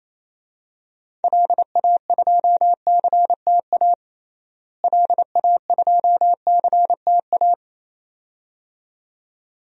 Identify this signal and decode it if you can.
Morse code, heard as a keyed tone, about 28 wpm, 700 Hz. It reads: LA2CTA LA2CTA
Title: DXCC-100 Call Signs - 6 Characters - Repeated 2x - 28wpm